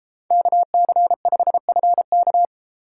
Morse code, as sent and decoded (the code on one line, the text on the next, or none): -.- -.-. ..... ..-. -.-
KC5FK